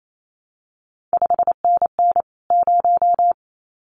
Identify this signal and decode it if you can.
Morse code, heard as a keyed tone, about 28 wpm, 700 Hz. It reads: 5NN 0
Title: CQ WW DX Contest Exchange Encore - 28wpm